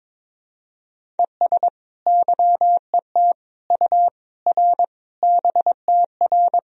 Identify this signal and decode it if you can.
Morse code, heard as a keyed tone, about 22 wpm, 700 Hz. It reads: ES YET U R BTR
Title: Sentences from Top 500 Words and Abbreviations 22wpm